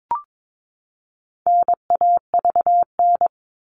Morse code, heard as a keyed tone, about 22 wpm, 700 Hz. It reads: NAVN
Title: All Cut Numbers - 4-Digit 22wpm